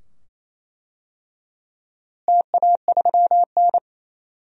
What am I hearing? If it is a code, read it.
Morse code, heard as a keyed tone, about 28 wpm, 700 Hz. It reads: TA3N
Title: DXCC-100 Call Signs - 4 Characters - Repeated 2x - 28wpm